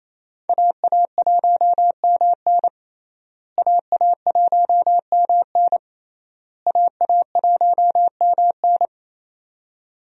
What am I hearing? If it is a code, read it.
Morse code, heard as a keyed tone, about 28 wpm, 700 Hz. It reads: AA1MN AA1MN AA1MN
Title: USA Call Signs - 5 Characters - Repeated 3x - 28wpm